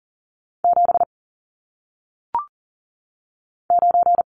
Morse code, heard as a keyed tone, about 39 wpm, 700 Hz. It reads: 7 9
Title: Numbers - 1-Digit 40wpm